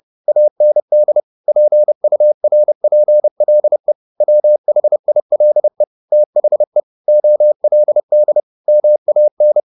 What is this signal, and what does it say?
Morse code, heard as a keyed tone, about 30 wpm, 600 Hz. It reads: AND PURPLE WHILE THE OLD MAN
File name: the old man and the sea / the old man and the sea 1 30wpm